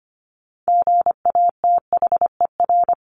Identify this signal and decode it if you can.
Morse code, heard as a keyed tone, about 25 wpm, 700 Hz. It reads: GATHER
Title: Top 1501-1750 Words 25wpm